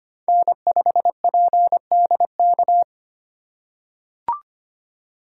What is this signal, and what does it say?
Morse code, heard as a keyed tone, about 25 wpm, 700 Hz. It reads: N5PDK
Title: USA Call Signs - 3x Character Spacing - 25wpm